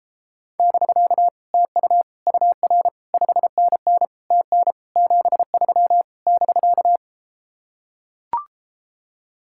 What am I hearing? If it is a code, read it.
Morse code, heard as a keyed tone, about 33 wpm, 700 Hz. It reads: <BK> TU UR 5NN TN 73 <BK>